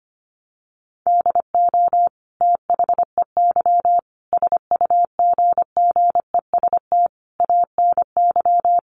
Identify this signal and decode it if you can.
Morse code, heard as a keyed tone, about 25 wpm, 700 Hz. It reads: DO THEY SUGGEST ANY
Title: Sentences from Top 1300 Words 25wpm